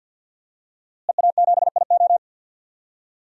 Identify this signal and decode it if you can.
Morse code, heard as a keyed tone, about 52 wpm, 700 Hz. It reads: EA7IO
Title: DXCC-100 Call Signs - 5 Characters - Repeated 3x - 50wpm